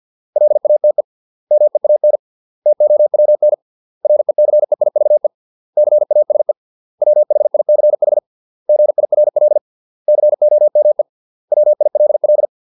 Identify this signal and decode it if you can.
Morse code, heard as a keyed tone, about 50 wpm, 600 Hz. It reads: LATE MEAN TOWN RECEIVE CASE WHICH GIRL COME WILL
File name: 632 words / 632 Words 50WPM